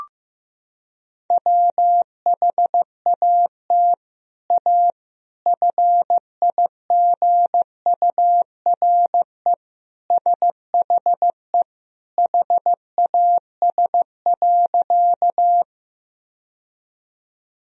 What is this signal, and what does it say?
Morse code, heard as a keyed tone, about 15 wpm, 700 Hz. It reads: WHAT A FIGURE SHE HAS.
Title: Sentences from Top 1000 Words - Review Part 2 - 15wpm